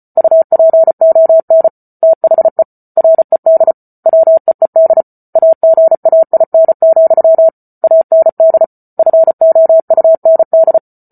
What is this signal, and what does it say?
Morse code, heard as a keyed tone, about 34 wpm, 650 Hz. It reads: UPON THE RED WEED AGAIN, AND FOUND